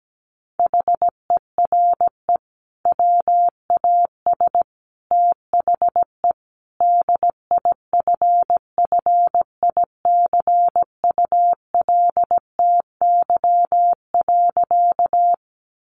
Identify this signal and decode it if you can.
Morse code, heard as a keyed tone, about 17 wpm, 700 Hz. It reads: HERE WAS THE DIFFICULTY.
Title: Sentences from Top 1500 Words 17wpm